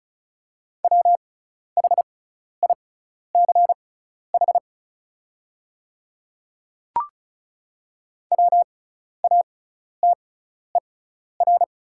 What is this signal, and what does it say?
Morse code, heard as a keyed tone, about 36 wpm, spaced as for 12 wpm, 690 Hz. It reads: WHICH WATER
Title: 6x Character Spacing - Top 100 Words - 35wpm